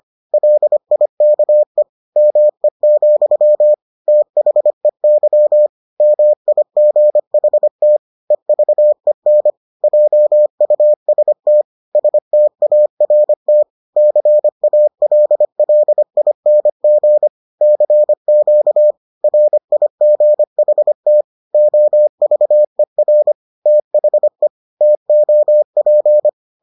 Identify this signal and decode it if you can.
Morse code, heard as a keyed tone, about 25 wpm, 600 Hz. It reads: LIKE ME, THEY MIGHT EVEN JUST START CALLING CQ RIGHT OVER THE TOP